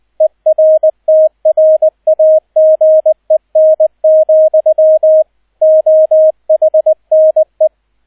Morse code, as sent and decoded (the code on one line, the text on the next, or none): . .-. - .-. .- --. -. --..-- / --- .... -. .
ERTRAGEN, OHNE